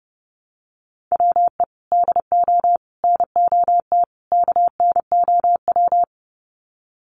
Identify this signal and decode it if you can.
Morse code, heard as a keyed tone, about 30 wpm, 700 Hz. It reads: WE DO NOT KNOW
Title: Sets of 4 Words 30wpm